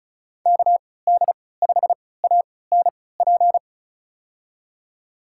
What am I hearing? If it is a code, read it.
Morse code, heard as a keyed tone, about 34 wpm, spaced as for 20 wpm, 700 Hz. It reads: KD5ANP
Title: USA Call Signs - 6 Characters - 3x Character Spacing - 35wpm